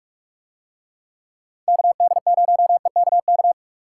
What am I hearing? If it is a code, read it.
Morse code, heard as a keyed tone, about 47 wpm, 700 Hz. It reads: KD0EKK